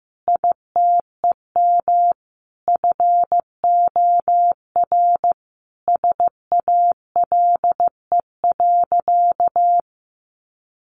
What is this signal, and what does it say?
Morse code, heard as a keyed tone, about 14 wpm, 700 Hz. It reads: ITEM FOR SALE.
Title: Sentences from Top 1400 Words 15wpm